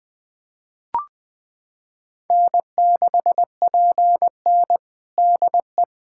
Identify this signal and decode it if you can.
Morse code, heard as a keyed tone, about 20 wpm, 700 Hz. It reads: N6PN DE
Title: Practice QSO - Exchange 4 - Ending - 20wpm